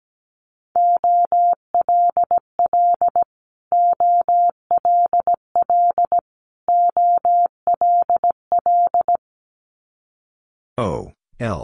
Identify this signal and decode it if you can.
Morse code, heard as a keyed tone, about 17 wpm, 700 Hz. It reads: OLL OLL OLL
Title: Multi-letter Phonemes Repeated 3x 17wpm